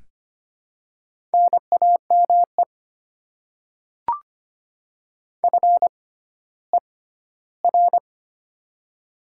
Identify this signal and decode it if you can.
Morse code, heard as a keyed tone, about 25 wpm, spaced as for 14 wpm, 700 Hz. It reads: NAME FER
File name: Top 20 QSO Elements - 6x Character Spacing - 25wpm